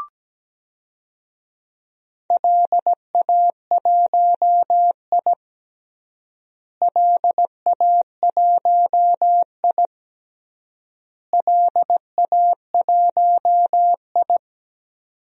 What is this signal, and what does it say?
Morse code, heard as a keyed tone, about 17 wpm, 700 Hz. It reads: LA1I LA1I LA1I
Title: DXCC-100 Call Signs - 4 Characters - Repeated 3x - 17wpm